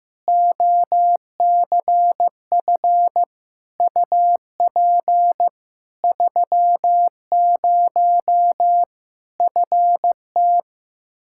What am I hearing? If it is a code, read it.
Morse code, heard as a keyed tone, about 16 wpm, 700 Hz. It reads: OCF UP 30 FT
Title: Practice QSO - Exchange 2 - Info - 15wpm